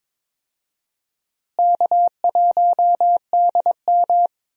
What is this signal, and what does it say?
Morse code, heard as a keyed tone, about 21 wpm, 700 Hz. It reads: K1DM
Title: USA Call Signs - 4 Characters - Repeated 2x - 22wpm